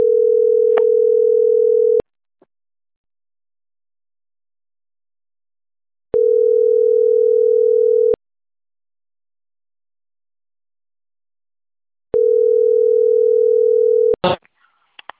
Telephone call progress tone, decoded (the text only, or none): ringback tone